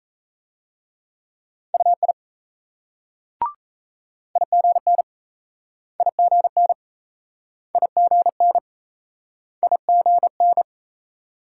Morse code, heard as a keyed tone, about 38 wpm, 700 Hz. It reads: UI IGN IGN IGN IGN